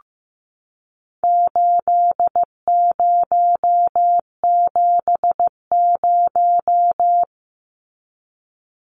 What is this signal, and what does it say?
Morse code, heard as a keyed tone, about 15 wpm, 700 Hz. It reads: 8070
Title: Numbers - 4-Digit 15wpm